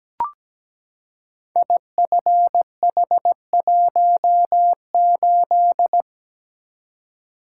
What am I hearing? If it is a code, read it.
Morse code, heard as a keyed tone, about 17 wpm, 700 Hz. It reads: IFH18
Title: Code Groups - 5 Letter:Numbers - 17wpm